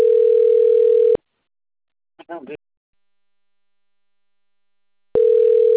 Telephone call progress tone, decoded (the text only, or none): ringback tone